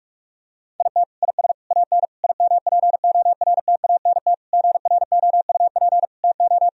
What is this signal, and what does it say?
Morse code, heard as a keyed tone, about 41 wpm, 700 Hz. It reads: IT IS AN IMPORTANT GROUP TO
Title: Sentences from Top 1000 Words - Review Part 1 - 45wpm